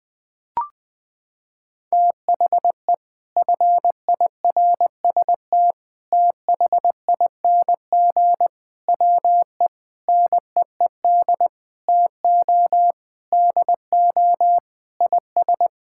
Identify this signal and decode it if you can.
Morse code, heard as a keyed tone, about 20 wpm, 700 Hz. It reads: THE FIRST THING WE NEED TO DO IS
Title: Sentences with 8-Word Phrases - 20wpm